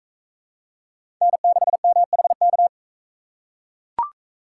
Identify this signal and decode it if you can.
Morse code, heard as a keyed tone, about 40 wpm, 700 Hz. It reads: N6MHK